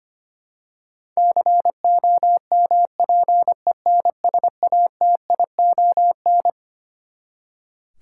Morse code, heard as a keyed tone, about 25 wpm, 700 Hz. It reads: COMPENSATION
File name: Twelve Letter Words - 25wpm